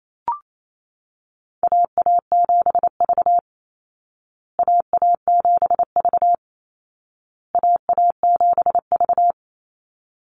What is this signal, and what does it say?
Morse code, heard as a keyed tone, about 28 wpm, 700 Hz. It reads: AA7V AA7V AA7V
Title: Call Signs Repeated 2x - 28wpm